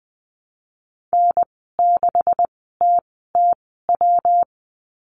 Morse code, heard as a keyed tone, about 20 wpm, spaced as for 14 wpm, 700 Hz. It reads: N6TTW
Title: 140 Lesson 23g - Letter f - Call Signs 20wpm